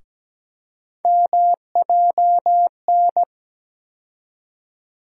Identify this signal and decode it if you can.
Morse code, heard as a keyed tone, about 17 wpm, 700 Hz. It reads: MJN
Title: USA Call Sign Postfixes - 3 Letters 17wpm